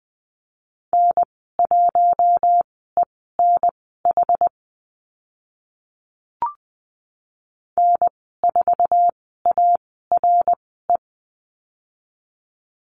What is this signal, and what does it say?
Morse code, heard as a keyed tone, about 20 wpm, spaced as for 15 wpm, 700 Hz. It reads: N1ENH N4ARE
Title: 062 Lesson 11g - Letter h - Call Signs 20wpm